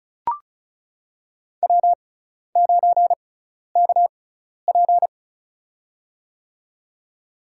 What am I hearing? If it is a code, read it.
Morse code, heard as a keyed tone, about 34 wpm, spaced as for 12 wpm, 700 Hz. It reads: W9KP
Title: USA Call Signs - 4 Characters - 6x Character Spacing - 35wpm